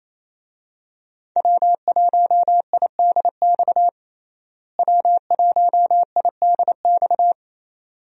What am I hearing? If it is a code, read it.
Morse code, heard as a keyed tone, about 27 wpm, 700 Hz. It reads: W1IDX W1IDX
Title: USA Call Signs - 5 Characters - Repeated 2x - 28wpm